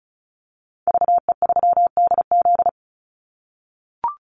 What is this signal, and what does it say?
Morse code, heard as a keyed tone, about 34 wpm, 700 Hz. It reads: VE3DZ
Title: Call Signs - 35wpm